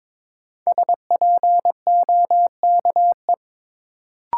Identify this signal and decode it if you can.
Morse code, heard as a keyed tone, about 22 wpm, 700 Hz. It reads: SPOKE
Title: Sentences from Top 1200 Words - New Word Drill - 22wpm